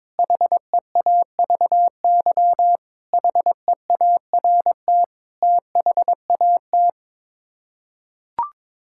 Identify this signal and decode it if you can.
Morse code, heard as a keyed tone, about 21 wpm, 700 Hz. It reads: HEAVY HEART THAT
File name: practice / 22wpm / Sets of 7 Words - 22wpm